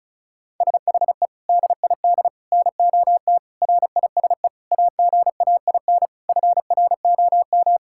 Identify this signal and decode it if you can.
Morse code, heard as a keyed tone, about 36 wpm, 700 Hz. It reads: SHE DID NOT RISE AGAIN FROM
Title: Sentences from Top 1000 Words - Review Part 6 - 35wpm